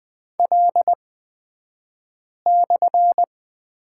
Morse code, heard as a keyed tone, about 20 wpm, spaced as for 5 wpm, 700 Hz. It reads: L/
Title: 254 Lesson 41d - Period - Two Characters 20wpm